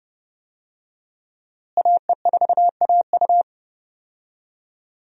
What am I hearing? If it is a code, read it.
Morse code, heard as a keyed tone, about 30 wpm, 700 Hz. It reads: AE4AU
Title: USA Call Signs - 5 Characters - Repeated 2x - 30wpm